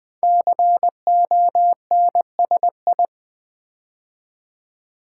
Morse code, heard as a keyed tone, about 20 wpm, 700 Hz. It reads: CONSI
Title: Build A Word - 5 Letters At A Time - 12-Letter Words - 20wpm